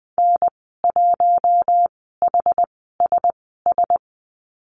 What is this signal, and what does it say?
Morse code, heard as a keyed tone, about 20 wpm, spaced as for 15 wpm, 700 Hz. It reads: N1HSS
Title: 088 Lesson 15g - Number 5 - Call Signs 20wpm